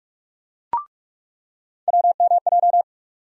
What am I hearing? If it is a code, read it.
Morse code, heard as a keyed tone, about 44 wpm, 700 Hz. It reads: WMJ
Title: USA Call Sign Postfixes - 3 Letters 45wpm